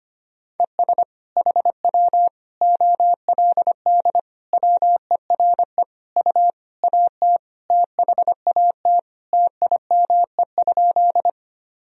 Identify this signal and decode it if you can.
Morse code, heard as a keyed tone, about 25 wpm, 700 Hz. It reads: ES HW OLD WERE U AT THAT TIME?
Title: Sentences from Top 200 Words and Abbreviations 25wpm